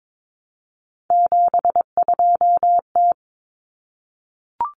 Morse code, heard as a keyed tone, about 22 wpm, 700 Hz. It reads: 72T
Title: Common Cut Numbers - 3-Digit 22wpm